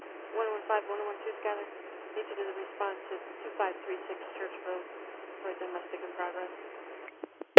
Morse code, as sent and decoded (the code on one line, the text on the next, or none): none